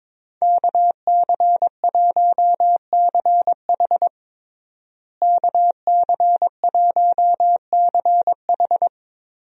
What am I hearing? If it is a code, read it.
Morse code, heard as a keyed tone, about 22 wpm, 700 Hz. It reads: KC1CH KC1CH
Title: USA Call Signs - 5 Characters - Repeated 3x - 22wpm